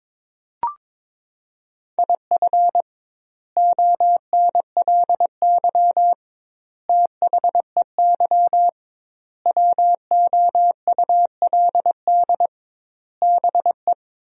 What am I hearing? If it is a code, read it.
Morse code, heard as a keyed tone, about 22 wpm, 700 Hz. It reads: IF ONLY THEY WOULD BE
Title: 2x Word-Spacing - Sentences from Top 500 Words - 22wpm